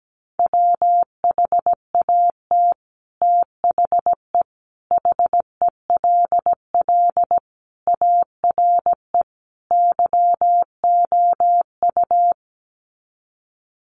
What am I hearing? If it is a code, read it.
Morse code, heard as a keyed tone, about 17 wpm, 700 Hz. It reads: WHAT THE HELL ARE YOU
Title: Sets of 5 Words Encore 17wpm